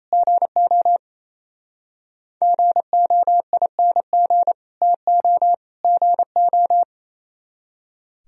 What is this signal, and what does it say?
Morse code, heard as a keyed tone, about 26 wpm, 700 Hz. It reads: GO GOING TO GO
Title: Speed-Racing - Sets of 3 Words 28wpm